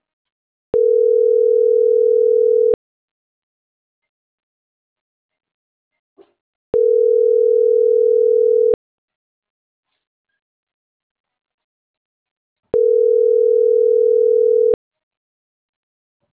Ringback tone (North America), 440 + 480 Hz.